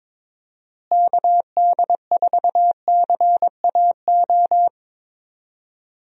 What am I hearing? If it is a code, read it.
Morse code, heard as a keyed tone, about 22 wpm, 700 Hz. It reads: KD4CAO
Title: USA Call Signs - 6 Characters - Repeated 2x - 22wpm